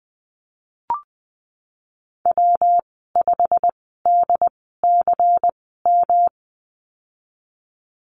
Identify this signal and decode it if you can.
Morse code, heard as a keyed tone, about 20 wpm, spaced as for 15 wpm, 700 Hz. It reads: W5DCM
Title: 140 Lesson 23g - Letter f - Call Signs 20wpm